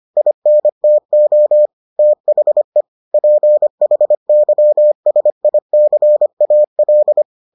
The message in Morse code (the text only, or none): INTO THE PHYSICAL